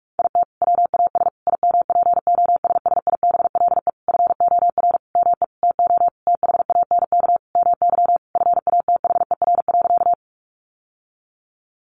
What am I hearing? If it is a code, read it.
Morse code, heard as a keyed tone, about 44 wpm, 700 Hz. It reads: IT WAS IMPOSSIBLE FOR ME TO THANK MY FATHER.
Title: Sentences from Top 1100 Words 45wpm